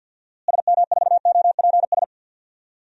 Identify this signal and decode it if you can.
Morse code, heard as a keyed tone, about 49 wpm, 700 Hz. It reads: SM4OPS
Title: DXCC-100 Call Signs - 6 Characters - Repeated 3x - 50wpm